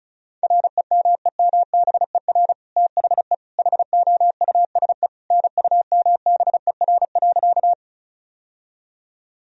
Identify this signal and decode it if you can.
Morse code, heard as a keyed tone, about 34 wpm, 700 Hz. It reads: REMEMBER THE HOUSE NUMBER.
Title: Sentences from Top 1000 Words - Review Part 1 - 35wpm